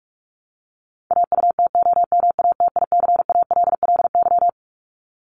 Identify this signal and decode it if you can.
Morse code, heard as a keyed tone, about 44 wpm, 700 Hz. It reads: AUTOMATICALLY
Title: Top 1751-2000 Words 45wpm